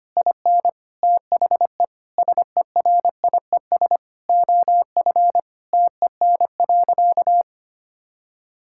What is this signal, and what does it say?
Morse code, heard as a keyed tone, about 25 wpm, 700 Hz. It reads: IN THE SERIES OF TEN.
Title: Sentences from Top 1000 Words - Review Part 5 - 25wpm